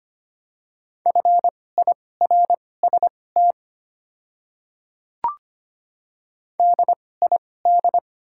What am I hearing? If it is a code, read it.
Morse code, heard as a keyed tone, about 25 wpm, spaced as for 18 wpm, 700 Hz. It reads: FIRST DID
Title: 2x Character Spacing - Top 100 Words - 25wpm